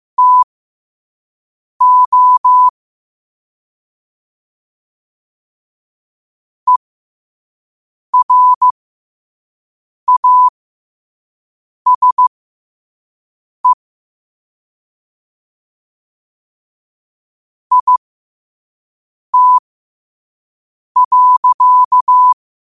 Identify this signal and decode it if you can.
Morse code, heard as a keyed tone, about 15 wpm, spaced as for 5 wpm, 1000 Hz. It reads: TO ERASE IT.
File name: qotdfiles / 5wpm